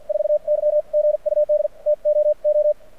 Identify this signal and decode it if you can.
Morse code, heard as a keyed tone, about 53 wpm, spaced as for 44 wpm, 600 Hz. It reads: 4Y GUD TOO
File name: archived / CRSnet